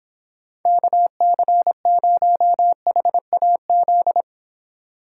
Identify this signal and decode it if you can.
Morse code, heard as a keyed tone, about 24 wpm, 700 Hz. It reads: KC0HAZ